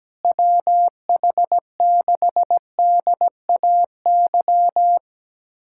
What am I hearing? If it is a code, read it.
Morse code, heard as a keyed tone, about 17 wpm, 700 Hz. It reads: WH6DAY
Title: USA Callsign DE Callsign - 6 Characters - 17wpm